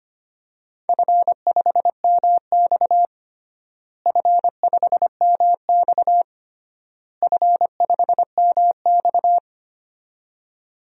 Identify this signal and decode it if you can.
Morse code, heard as a keyed tone, about 25 wpm, 700 Hz. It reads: F5MX F5MX F5MX
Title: DXCC-100 Call Signs - 4 Characters - Repeated 3x - 25wpm